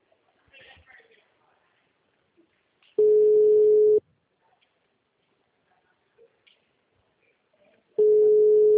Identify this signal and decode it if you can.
Ringback tone (Europe, 425 Hz), with speech.